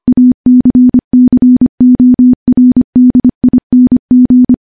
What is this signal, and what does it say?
Morse code, heard as a keyed tone, about 25 wpm, 250 Hz. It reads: ACCORDING
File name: morse / morse-training-25WPM